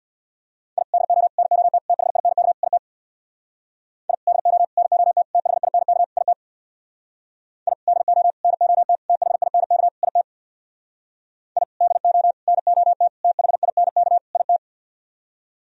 Morse code, heard as a keyed tone, about 63 wpm, 700 Hz. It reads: I DO NOT THINK IT I DO NOT THINK IT I DO NOT THINK IT I DO NOT THINK IT